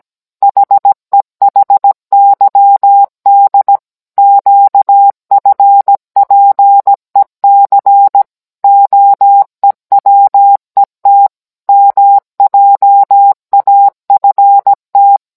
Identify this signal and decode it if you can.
Morse code, heard as a keyed tone, about 17 wpm, 800 Hz. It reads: HEHYD QFPEC OEWET MJAFT